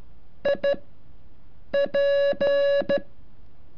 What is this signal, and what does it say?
Morse code, heard as a keyed tone, about 12 wpm, spaced as for 6 wpm, 560 Hz. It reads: IP
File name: curso / ejercicio 33